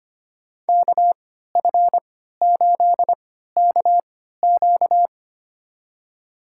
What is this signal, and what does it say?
Morse code, heard as a keyed tone, about 25 wpm, spaced as for 14 wpm, 700 Hz. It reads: KF8KQ
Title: USA Call Signs - 5 Characters - 3x Character Spacing - 25wpm